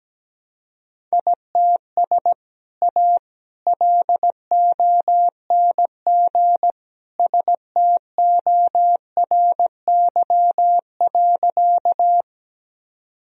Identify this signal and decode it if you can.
Morse code, heard as a keyed tone, about 17 wpm, 700 Hz. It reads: ITS A LONG STORY.
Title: Sentences from Top 1000 Words - Review Part 1 - 17wpm